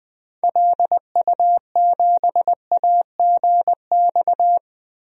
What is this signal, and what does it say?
Morse code, heard as a keyed tone, about 21 wpm, 700 Hz. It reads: LU7AGX
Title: DXCC-100 Call Signs - 6 Characters - Repeated 3x - 20wpm